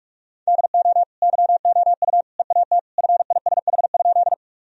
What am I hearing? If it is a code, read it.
Morse code, heard as a keyed tone, about 46 wpm, 700 Hz. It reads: DO YOU EAT FISH?
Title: Sentences from Top 1000 Words - Review Part 1 - 45wpm